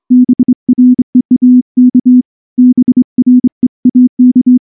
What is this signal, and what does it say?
Morse code, heard as a keyed tone, about 24 wpm, 250 Hz. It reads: BREAK BREAK
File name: morse-training-25WPM